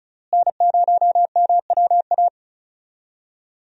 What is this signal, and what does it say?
Morse code, heard as a keyed tone, about 34 wpm, 700 Hz. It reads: N0MWA
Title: USA Call Signs - 5 Characters - Repeated 2x - 35wpm